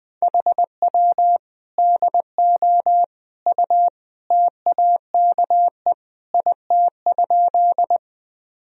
Morse code, heard as a keyed tone, about 20 wpm, 700 Hz. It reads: HW DO U TAKE IT?
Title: Sentences from Top 200 Words and Abbreviations 20wpm